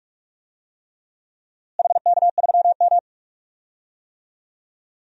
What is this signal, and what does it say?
Morse code, heard as a keyed tone, about 46 wpm, 700 Hz. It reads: HK3M
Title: DXCC-100 Call Signs - 4 Characters - Repeated 3x - 45wpm